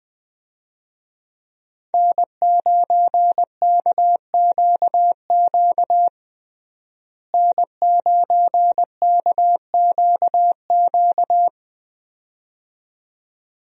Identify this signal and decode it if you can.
Morse code, heard as a keyed tone, about 20 wpm, 700 Hz. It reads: N9KQQ N9KQQ
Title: USA Call Signs - 5 Characters - Repeated 2x - 20wpm